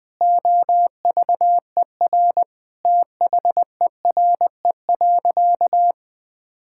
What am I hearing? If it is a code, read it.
Morse code, heard as a keyed tone, about 21 wpm, 700 Hz. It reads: OVER THERE.
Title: Sentences from Top 1600 Words 20wpm